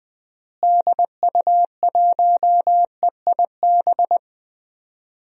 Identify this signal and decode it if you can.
Morse code, heard as a keyed tone, about 20 wpm, 700 Hz. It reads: DU1EIB
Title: DXCC-100 Call Signs - 6 Characters - Repeated 3x - 20wpm